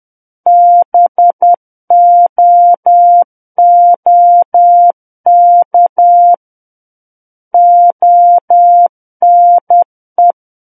Morse code, heard as a keyed tone, about 10 wpm, 700 Hz. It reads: BOOK ONE
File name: Book One / 01-Chapter 1 - 10 WPM